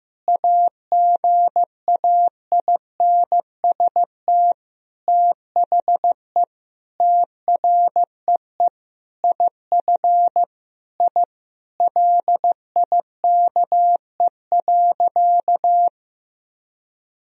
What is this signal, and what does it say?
Morse code, heard as a keyed tone, about 16 wpm, 700 Hz. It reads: AGAINST THE TREE IF I LIKE.
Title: Sentences from Top 1000 Words - Review Part 1 - 15wpm